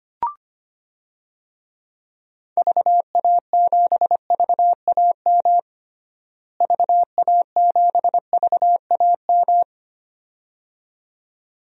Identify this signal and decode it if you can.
Morse code, heard as a keyed tone, about 25 wpm, 700 Hz. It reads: VA7VAM VA7VAM
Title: DXCC-100 Call Signs - 6 Characters - Repeated 2x - 25wpm